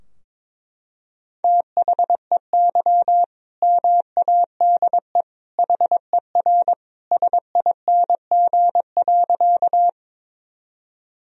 Morse code, heard as a keyed tone, about 21 wpm, 700 Hz. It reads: THEY MADE HER SING.
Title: Sentences from Top 1000 Words - Review Part 2 - 22wpm